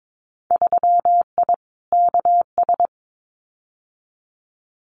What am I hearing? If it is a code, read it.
Morse code, heard as a keyed tone, about 22 wpm, 700 Hz. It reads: 3I KS
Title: Winter Field Day Exchanges 22wpm